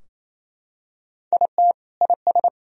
Morse code, heard as a keyed tone, about 28 wpm, 700 Hz. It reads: IT IS
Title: Sentences from Top 1000 Words - Review Part 4 - 28wpm